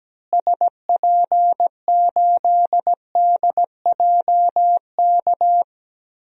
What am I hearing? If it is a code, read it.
Morse code, heard as a keyed tone, about 17 wpm, 700 Hz. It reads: SP8DJK
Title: DXCC-100 Call Signs - 6 Characters - Repeated 3x - 17wpm